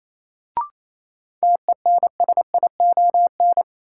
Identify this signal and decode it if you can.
Morse code, heard as a keyed tone, about 28 wpm, 700 Hz. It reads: TENSION
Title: Sentences from Top 1600 Words - New Word Drill Alt - 28wpm